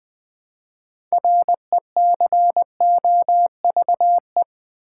Morse code, heard as a keyed tone, about 20 wpm, 700 Hz. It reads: RECOVE
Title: Build A Word - 2 Letters At A Time - 8-Letter Words - 20wpm